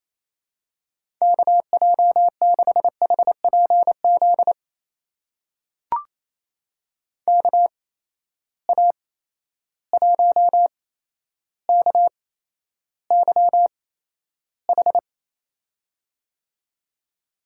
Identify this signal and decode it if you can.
Morse code, heard as a keyed tone, about 28 wpm, spaced as for 12 wpm, 700 Hz. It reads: KJ6HPZ KA1KYH
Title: USA Call Signs - 6 Characters - 8x Character Spacing - 28wpm